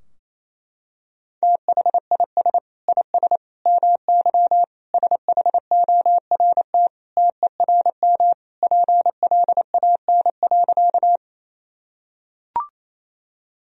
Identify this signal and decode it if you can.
Morse code, heard as a keyed tone, about 28 wpm, 700 Hz. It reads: THIS IS MY SHORT TERM PLAN.
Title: Sentences from Top 1000 Words - Review Part 4 - 28wpm